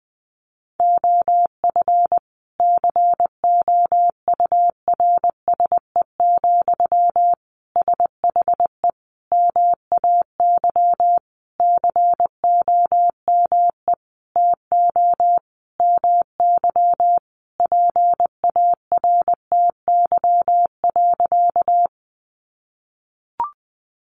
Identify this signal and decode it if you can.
Morse code, heard as a keyed tone, about 20 wpm, 700 Hz. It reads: OF COURSE, SHE MAY COME TO MY PARTY.
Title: Sentences from Top 1400 Words 20wpm